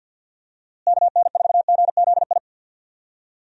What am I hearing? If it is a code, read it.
Morse code, heard as a keyed tone, about 46 wpm, 700 Hz. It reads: KN4GZI
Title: USA Call Signs - 6 Characters - Repeated 3x - 50wpm